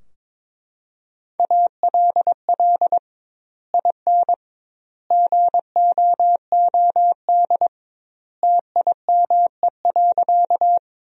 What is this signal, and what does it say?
Morse code, heard as a keyed tone, about 22 wpm, 700 Hz. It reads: ALL IN GOOD TIME.
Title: 2x Word-Spacing - Sentences from Top 200 Words - 22wpm